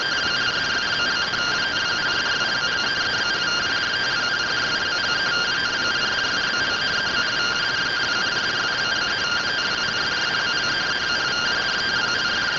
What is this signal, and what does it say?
Two-tone teleprinter signal, about 170 Hz shift.
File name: findings / CORONAL fr69 nb202 3mm.gen-39 rtty rx